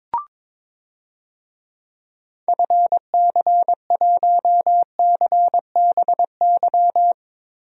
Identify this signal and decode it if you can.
Morse code, heard as a keyed tone, about 22 wpm, 700 Hz. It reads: FC1CBY
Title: DXCC-100 Call Signs - 6 Characters - Repeated 2x - 22wpm